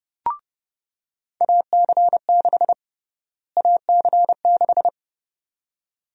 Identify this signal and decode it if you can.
Morse code, heard as a keyed tone, about 30 wpm, 700 Hz. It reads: AC6 AC6
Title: USA Call Sign Prefixes - Repeated 2x - 30wpm